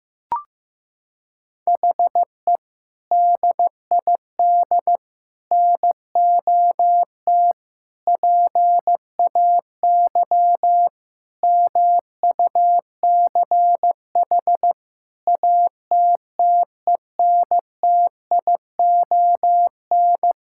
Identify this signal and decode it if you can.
Morse code, heard as a keyed tone, about 15 wpm, 700 Hz. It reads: HE DID NOT PAY MUCH ATTENTION
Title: Sentences from Top 1000 Words - Review Part 6 - 15wpm